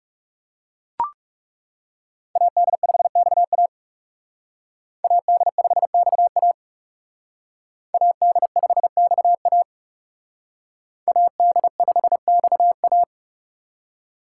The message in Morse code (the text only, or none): AD5XA AD5XA AD5XA AD5XA